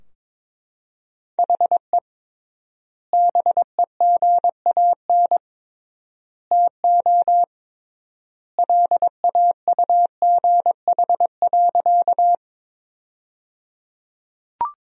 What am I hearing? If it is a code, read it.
Morse code, heard as a keyed tone, about 22 wpm, 700 Hz. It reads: HE BEGAN TO LAUGH.
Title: 3x Word-Spacing - Sentences from Top 500 Words - 22wpm